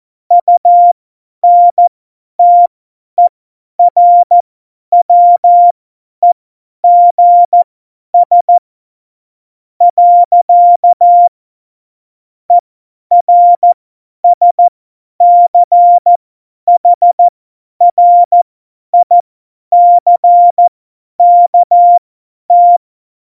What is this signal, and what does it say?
Morse code, heard as a keyed tone, about 14 wpm, spaced as for 10 wpm, 700 Hz. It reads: UNTERWEGS. ERSCHRICKT
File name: Morsekurs / Witze / Witz 28. August 2018, 50 F70